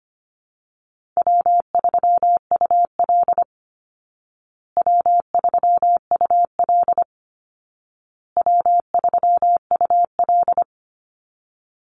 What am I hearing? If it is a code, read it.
Morse code, heard as a keyed tone, about 25 wpm, 700 Hz. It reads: W3UL W3UL W3UL
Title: Easy Call Signs Repeated 3x - 25wpm